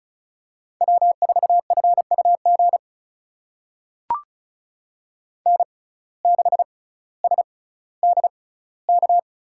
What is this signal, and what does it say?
Morse code, heard as a keyed tone, about 36 wpm, spaced as for 18 wpm, 700 Hz. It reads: W4FUG N6SDK
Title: USA Call Signs - 5 Characters - 6x Character Spacing - 35wpm